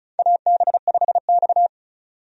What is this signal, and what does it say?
Morse code, heard as a keyed tone, about 35 wpm, 700 Hz. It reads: AB5X
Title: USA Call Signs - 4 Characters - 35wpm